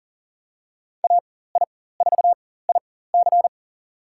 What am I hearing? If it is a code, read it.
Morse code, heard as a keyed tone, about 40 wpm, spaced as for 19 wpm, 700 Hz. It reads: AI4IC
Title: USA Call Signs - 5 Characters - 4x Character Spacing - 40wpm